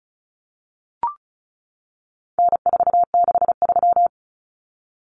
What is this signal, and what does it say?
Morse code, heard as a keyed tone, about 34 wpm, 700 Hz. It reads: N463
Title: Common Cut Numbers - 4-Digit 35wpm